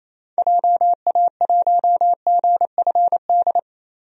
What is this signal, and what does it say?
Morse code, heard as a keyed tone, about 28 wpm, 700 Hz. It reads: JA1GFD
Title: DXCC-100 Call Signs - 6 Characters - Repeated 3x - 28wpm